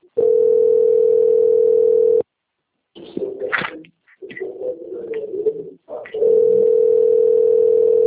Ringback tone (North America), 440 + 480 Hz.
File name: out- - - - - .19924